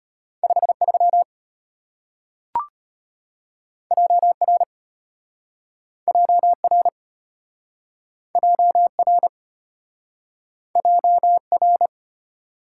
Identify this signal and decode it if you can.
Morse code, heard as a keyed tone, about 33 wpm, 700 Hz. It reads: 53 JR JR JR JR